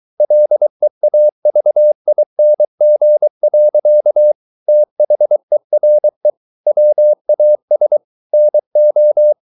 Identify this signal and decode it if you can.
Morse code, heard as a keyed tone, about 23 wpm, 600 Hz. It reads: LEAVING. THERE WAS NO